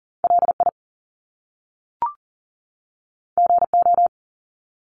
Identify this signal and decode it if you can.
Morse code, heard as a keyed tone, about 40 wpm, 700 Hz. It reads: LI GO
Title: Top Two Letter Combinations in Words 40wpm